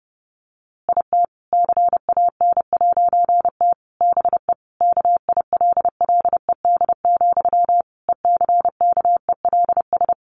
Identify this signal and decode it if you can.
Morse code, heard as a keyed tone, about 28 wpm, 700 Hz. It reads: IT CAN'T BE KILLED, ECKELS
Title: A Sound Of Thunder 30wpm